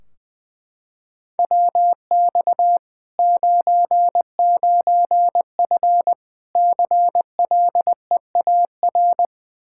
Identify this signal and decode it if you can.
Morse code, heard as a keyed tone, about 21 wpm, 700 Hz. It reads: WX 99F CLEAR
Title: Practice QSO - Exchange 2 - Info - 20wpm